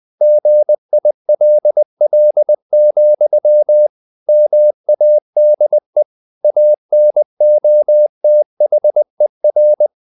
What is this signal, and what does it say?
Morse code, heard as a keyed tone, about 20 wpm, 600 Hz. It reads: GILL, MADE ANOTHER
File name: the old man and the sea part 2 20wpm